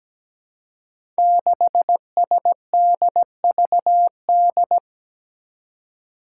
Morse code, heard as a keyed tone, about 17 wpm, 700 Hz. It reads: 6SDVD
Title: Five Challenging Letter-Number Combinations 17wpm